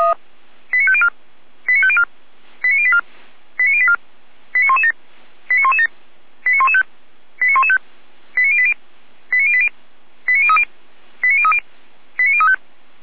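Five-tone selective call (ZVEI), heard as quick runs of five tones, tones of about 70 ms. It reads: 89504 89504 89084 89084 89117 89117 89115 89115 89080 89080 89030 89030 89035
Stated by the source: ZVEI five-tone selective call 89120 received via SDR.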